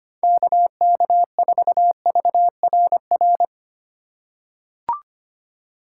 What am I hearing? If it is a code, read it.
Morse code, heard as a keyed tone, about 25 wpm, 700 Hz. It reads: KK4VRR
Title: USA Call Signs - 6 Characters - 2x Character Spacing - 25wpm